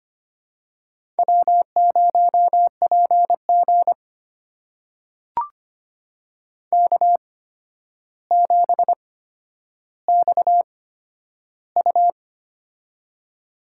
Morse code, heard as a keyed tone, about 25 wpm, spaced as for 11 wpm, 700 Hz. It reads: W0PG K7XU
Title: USA Call Signs - 4 Characters - 8x Character Spacing - 25wpm